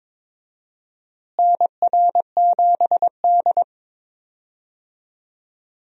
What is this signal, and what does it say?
Morse code, heard as a keyed tone, about 21 wpm, 700 Hz. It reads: NR7D
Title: USA Call Signs - 4 Characters - Repeated 3x - 22wpm